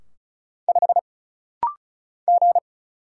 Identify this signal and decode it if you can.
Morse code, heard as a keyed tone, about 33 wpm, 700 Hz. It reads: 5 G
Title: Single Letter-Number Alt - Rapid-Fire 35wpm